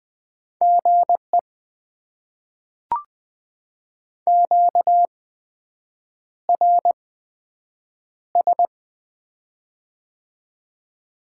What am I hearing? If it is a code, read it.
Morse code, heard as a keyed tone, about 20 wpm, spaced as for 7 wpm, 700 Hz. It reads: GE QRS
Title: Top 90 QSO Elements - 8x Character Spacing - 20wpm